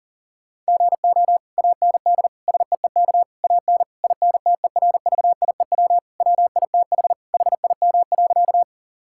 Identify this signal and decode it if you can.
Morse code, heard as a keyed tone, about 40 wpm, 700 Hz. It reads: GO AND SEEK AN INTERVIEW WITH HIM.
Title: Sentences from Top 1600 Words 40wpm